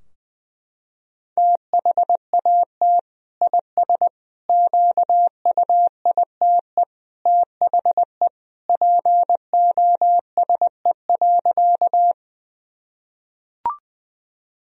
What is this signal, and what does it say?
Morse code, heard as a keyed tone, about 20 wpm, 700 Hz. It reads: THAT IS QUITE THE POSE.
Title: Sentences from Top 1000 Words - Review Part 2 - 20wpm